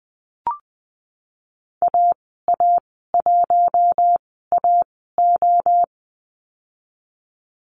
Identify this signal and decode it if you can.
Morse code, heard as a keyed tone, about 20 wpm, spaced as for 15 wpm, 700 Hz. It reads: AA1AO
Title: 076 Lesson 13g - Letter L - Call Signs 20wpm